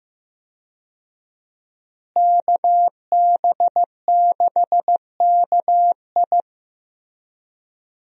Morse code, heard as a keyed tone, about 15 wpm, 700 Hz. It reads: KB6KI
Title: USA Call Signs - 5 Characters - Repeated 3x - 15wpm